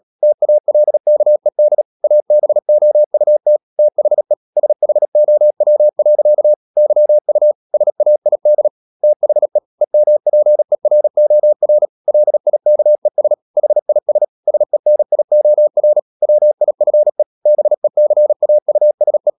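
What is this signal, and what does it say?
Morse code, heard as a keyed tone, about 37 wpm, 600 Hz. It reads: TALKED ABOUT THE SHOW. YU SAID THE EMPEROR LIKES HIS SENIOR WIFE BECAUSE
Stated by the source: Morse code practice at 35–39 wpm sending English text of Ken Follett's novel Never, from CW med Gnister.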